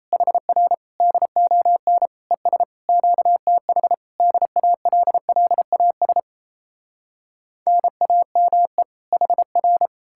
Morse code, heard as a keyed tone, about 38 wpm, 700 Hz. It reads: HR DON ES QTH DALLAS NAME HR